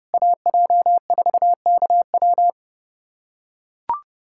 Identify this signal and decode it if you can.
Morse code, heard as a keyed tone, about 30 wpm, 700 Hz. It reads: AJ4KW
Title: USA Callsign DE Callsign - 5 Characters - 30wpm